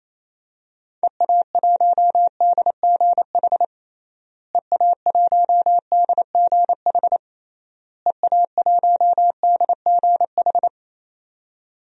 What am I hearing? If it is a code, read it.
Morse code, heard as a keyed tone, about 28 wpm, 700 Hz. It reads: EA1DGH EA1DGH EA1DGH
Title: DXCC-100 Call Signs - 6 Characters - Repeated 3x - 28wpm